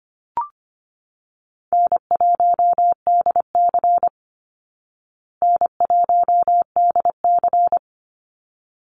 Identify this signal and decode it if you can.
Morse code, heard as a keyed tone, about 25 wpm, 700 Hz. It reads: N1DC N1DC
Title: Easy Call Signs Repeated 2x - 25wpm